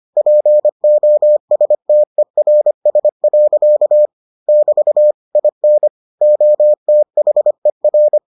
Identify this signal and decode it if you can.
Morse code, heard as a keyed tone, about 25 wpm, 600 Hz. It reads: POSTERS. <BT> IN OTHER